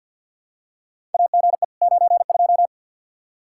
Morse code, heard as a keyed tone, about 51 wpm, 700 Hz. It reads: AGE 92
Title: Practice QSO - Exchange 3 - More Info - 50wpm